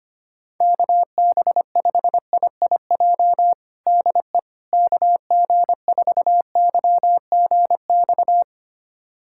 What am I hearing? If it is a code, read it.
Morse code, heard as a keyed tone, about 25 wpm, 700 Hz. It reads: KB5IIJ DE KG4YGX
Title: USA Callsign DE Callsign - 6 Characters - 25wpm